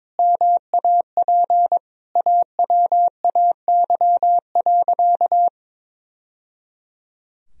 Morse code, heard as a keyed tone, about 22 wpm, 700 Hz. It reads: MAP AWAY.
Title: Sentences from Top 1000 Words - Review Part 2 - 22wpm